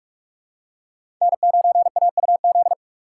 Morse code, heard as a keyed tone, about 45 wpm, 700 Hz. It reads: N9AUZ
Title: USA Call Signs - 5 Characters - 3x Character Spacing - 45wpm